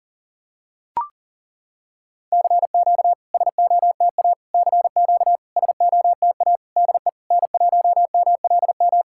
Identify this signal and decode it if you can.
Morse code, heard as a keyed tone, about 40 wpm, 690 Hz. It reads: CQ SOTA CQ SOTA DE N1MLM
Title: SOTA QSO Practice 40wpm